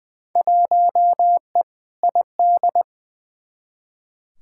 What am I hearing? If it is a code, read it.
Morse code, heard as a keyed tone, about 20 wpm, 700 Hz. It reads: 1E ID
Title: Field Day Exchanges - 20wpm